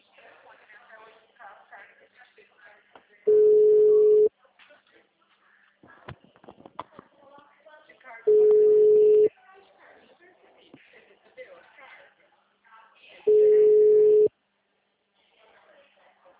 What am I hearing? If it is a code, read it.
Ringback tone (Europe, 425 Hz), with speech.
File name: out- -2015- - - .61